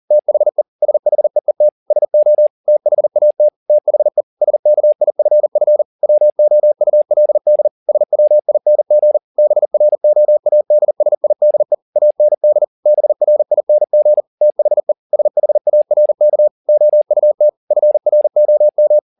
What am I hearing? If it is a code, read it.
Morse code, heard as a keyed tone, about 40 wpm, 600 Hz. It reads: THE SHEET SO THAT THE SKIFF WOULD SWING BROADSIDE AND BRING THE SHARK OUT FROM
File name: the old man and the sea 2 40wpm